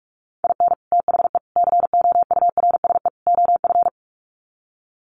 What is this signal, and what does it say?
Morse code, heard as a keyed tone, about 44 wpm, 700 Hz. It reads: IN THE COURSE OF
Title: Sets of 4 Words 45wpm